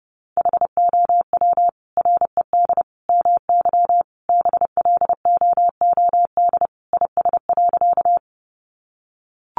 Morse code, heard as a keyed tone, about 30 wpm, 700 Hz. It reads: HOW RED MY BLOOD IS.
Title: Sentences from Top 1100 Words 30wpm